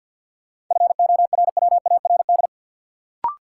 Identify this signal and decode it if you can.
Morse code, heard as a keyed tone, about 53 wpm, 700 Hz. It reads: FORWARD
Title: Sentences from Top 1300 Words - New Word Drill Alt - 50wpm